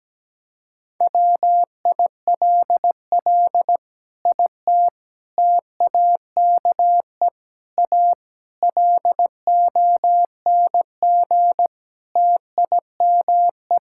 Morse code, heard as a keyed tone, about 17 wpm, 700 Hz. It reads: WILL IT TAKE A LONG TIME
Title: Sentences from Top 1000 Words - Review Part 6 - 17wpm